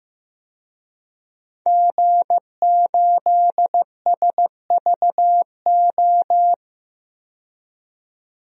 Morse code, heard as a keyed tone, about 16 wpm, 700 Hz. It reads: G8SVO
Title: DXCC-100 Call Signs - 5 Characters - 15wpm